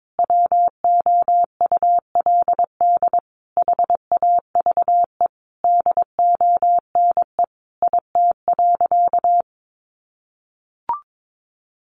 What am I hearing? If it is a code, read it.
Morse code, heard as a keyed tone, about 22 wpm, 700 Hz. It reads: WOULD HAVE DONE IT.
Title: Sentences from Top 1200 Words 22wpm